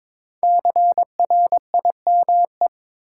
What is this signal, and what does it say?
Morse code, heard as a keyed tone, about 22 wpm, 700 Hz. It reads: CRIME
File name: 5 Letter Words - First 3 Letters Given - 22wpm